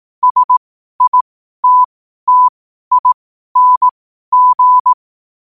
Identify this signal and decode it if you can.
Morse code, heard as a keyed tone, about 18 wpm, spaced as for 12 wpm, 1000 Hz. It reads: SITTING
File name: qotdfiles / 12wpm